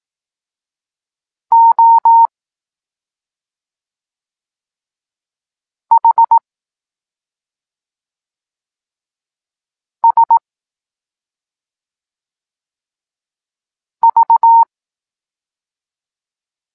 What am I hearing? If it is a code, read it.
Morse code, heard as a keyed tone, about 18 wpm, 920 Hz. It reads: O H S V
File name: Tekst-9-rownomierny